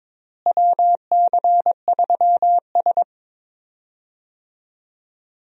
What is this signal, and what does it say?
Morse code, heard as a keyed tone, about 22 wpm, 700 Hz. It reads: WC3S
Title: USA Call Signs - 4 Characters - Repeated 3x - 22wpm